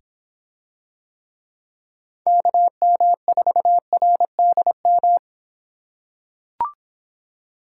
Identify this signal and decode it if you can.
Morse code, heard as a keyed tone, about 25 wpm, 700 Hz. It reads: KM4RDM